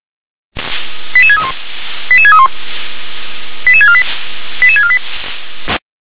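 Five-tone selective call (ZVEI), heard as quick runs of five tones, tones of about 70 ms. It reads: 88631 88631 88648 88648
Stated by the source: ZVEI five-tone selective call received on SDR.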